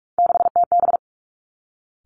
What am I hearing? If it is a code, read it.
Morse code, heard as a keyed tone, about 46 wpm, 700 Hz. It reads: 6TB
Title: Three Challenging Letter-Number Combinations 45wpm